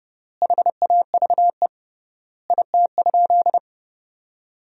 Morse code, heard as a keyed tone, about 30 wpm, 700 Hz. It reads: HAVE IT?
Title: 3x Word-Spacing - Sentences from Top 100 Words - 30wpm